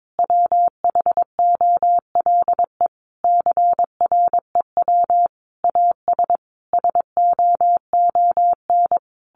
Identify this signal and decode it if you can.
Morse code, heard as a keyed tone, about 22 wpm, 700 Hz. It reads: WHOLE CREW AS SOON
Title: Sentences from Top 1500 Words 22wpm